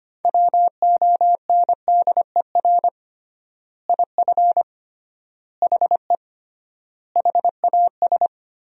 Morse code, heard as a keyed tone, about 25 wpm, 700 Hz. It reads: WONDER IF HE HAS
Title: 3x Word-Spacing - Sentences from Top 500 Words - 25wpm